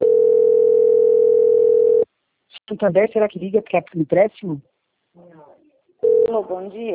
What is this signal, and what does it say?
Ringback tone (North America), 440 + 480 Hz, with speech.
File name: out- - - - - .9806